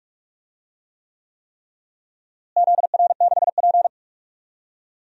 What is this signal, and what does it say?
Morse code, heard as a keyed tone, about 41 wpm, 700 Hz. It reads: ZR6P
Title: DXCC-100 Call Signs - 4 Characters - Repeated 3x - 45wpm